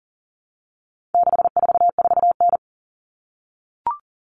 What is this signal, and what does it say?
Morse code, heard as a keyed tone, about 40 wpm, 700 Hz. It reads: 644N
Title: All Cut Numbers - 4-Digit 40wpm